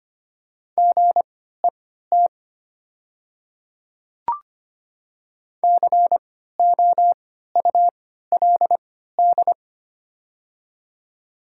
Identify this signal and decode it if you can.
Morse code, heard as a keyed tone, about 25 wpm, spaced as for 14 wpm, 700 Hz. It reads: GET COULD
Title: 3x Character Spacing - Top 500 Words - 25wpm